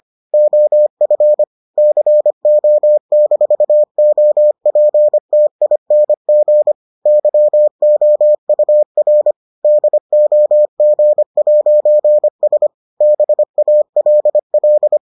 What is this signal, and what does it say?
Morse code, heard as a keyed tone, about 25 wpm, 600 Hz. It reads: OF CO-OPTING YOUR DOG'S BALL